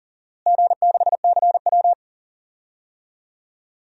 Morse code, heard as a keyed tone, about 39 wpm, 700 Hz. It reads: G6CW
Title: DXCC-100 Call Signs - 4 Characters - Repeated 3x - 40wpm